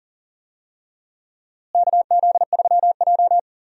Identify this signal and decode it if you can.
Morse code, heard as a keyed tone, about 37 wpm, 700 Hz. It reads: KZ3J
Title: USA Call Signs - 4 Characters - Repeated 3x - 40wpm